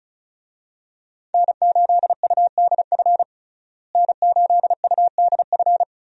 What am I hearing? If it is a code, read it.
Morse code, heard as a keyed tone, about 34 wpm, 700 Hz. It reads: N8UDF N8UDF
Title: USA Call Signs - 5 Characters - Repeated 2x - 35wpm